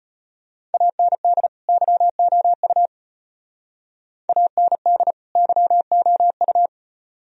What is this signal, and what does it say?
Morse code, heard as a keyed tone, about 38 wpm, 700 Hz. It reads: AND YOU AND YOU